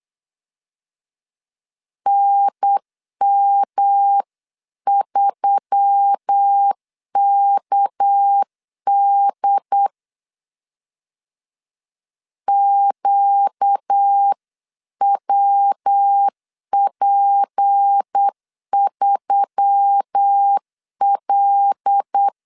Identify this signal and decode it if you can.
Morse code, heard as a keyed tone, about 8 wpm, 800 Hz. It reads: NM3KD QWP3L
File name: 8wpm